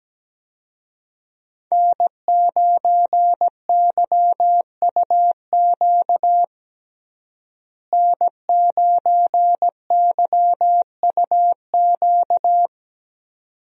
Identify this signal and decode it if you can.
Morse code, heard as a keyed tone, about 17 wpm, 700 Hz. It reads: N9YUQ N9YUQ
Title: USA Call Signs - 5 Characters - Repeated 2x - 17wpm